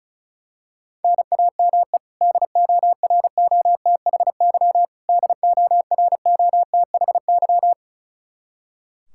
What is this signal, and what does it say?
Morse code, heard as a keyed tone, about 36 wpm, 700 Hz. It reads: NAME DOROTHY DOROTHY
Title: Practice QSO - Exchange 1 - Beginning - 35wpm